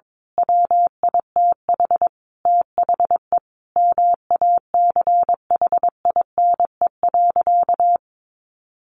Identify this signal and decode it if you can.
Morse code, heard as a keyed tone, about 22 wpm, 700 Hz. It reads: WITH THE MACHINE.
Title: A Sound Of Thunder 22wpm